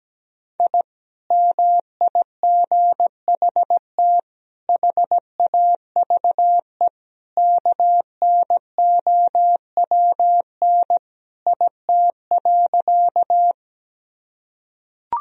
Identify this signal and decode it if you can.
Morse code, heard as a keyed tone, about 17 wpm, 700 Hz. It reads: I MIGHT HAVE KNOWN IT.
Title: Sentences from Top 1000 Words - Review Part 3 - 17wpm